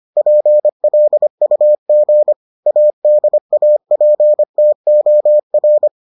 Morse code, heard as a keyed tone, about 25 wpm, 600 Hz. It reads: PLUG ADAPTOR